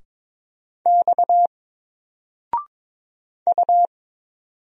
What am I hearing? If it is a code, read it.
Morse code, heard as a keyed tone, about 22 wpm, 700 Hz. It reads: X U
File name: Single-Letter-Number-22wpm